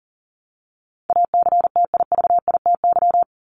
Morse code, heard as a keyed tone, about 40 wpm, 700 Hz. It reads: ACTIVITY
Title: ES:S Suffix - Top 100 Words with Base Words 40wpm